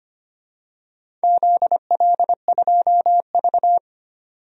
Morse code, heard as a keyed tone, about 25 wpm, 700 Hz. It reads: ZL2V
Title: DXCC-100 Call Signs - 4 Characters - 25wpm